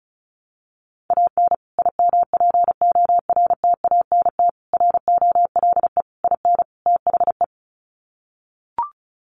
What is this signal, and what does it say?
Morse code, heard as a keyed tone, about 34 wpm, 700 Hz. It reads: AN IMPORTANT ROLE IN THE
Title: Sets of 5 Words Encore2 35wpm